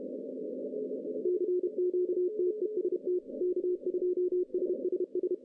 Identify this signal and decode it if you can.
Morse code, heard as a keyed tone, about 30 wpm, 370 Hz, with radio noise. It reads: CQTEST K2SSS
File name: 2015 cq ww cw / 2015 cqwww k5tr - 0-41